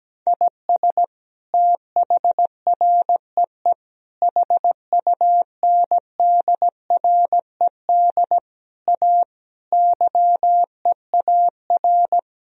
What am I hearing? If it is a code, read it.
Morse code, heard as a keyed tone, about 17 wpm, 700 Hz. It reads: IS THREE HUNDRED A YEAR
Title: Sentences from Top 1000 Words - Review Part 2 - 17wpm